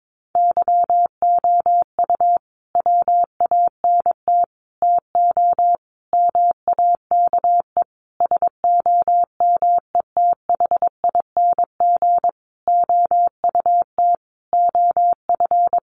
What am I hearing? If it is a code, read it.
Morse code, heard as a keyed tone, about 22 wpm, 700 Hz. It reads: YOU WANT TO MAKE SOMETHING OUT OF
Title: Sentences from Top 600 Words 22wpm